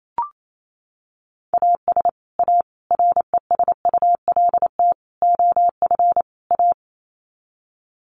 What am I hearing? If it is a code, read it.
Morse code, heard as a keyed tone, about 28 wpm, 700 Hz. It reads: AS A RESULT OF A
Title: Sets of 5 Words Encore 28wpm